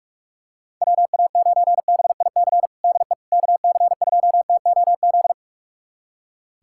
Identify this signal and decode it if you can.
Morse code, heard as a keyed tone, about 46 wpm, 700 Hz. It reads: WA9BIC DE KC2TOZ
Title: USA Callsign DE Callsign 45wpm